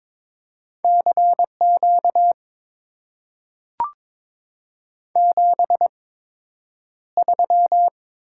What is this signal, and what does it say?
Morse code, heard as a keyed tone, about 22 wpm, spaced as for 10 wpm, 690 Hz. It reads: CQ 73
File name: Top 90 QSO Elements - 8x Character Spacing - 22wpm